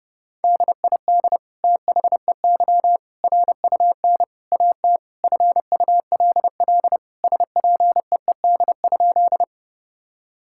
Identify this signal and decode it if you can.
Morse code, heard as a keyed tone, about 30 wpm, 700 Hz. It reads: DID THEY RUN AT FULL SPEED?
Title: Sentences from Top 1000 Words - Review Part 5 - 30wpm